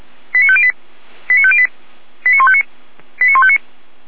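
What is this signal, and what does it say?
Five-tone selective call (ZVEI), heard as quick runs of five tones, tones of about 70 ms. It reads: 89508 89508 89159 89159
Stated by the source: ZVEI five-tone selective call received via SDR.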